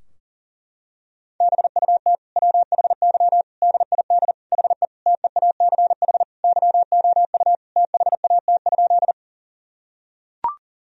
Morse code, heard as a keyed tone, about 40 wpm, 700 Hz. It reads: BUT WHY DID HE TEACH YOU THAT?
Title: Sentences from Top 1000 Words - Review Part 2 - 40wpm